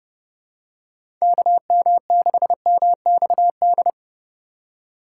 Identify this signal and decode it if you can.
Morse code, heard as a keyed tone, about 32 wpm, 700 Hz. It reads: KM6MXD
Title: USA Call Signs - 4x Character Spacing - 30wpm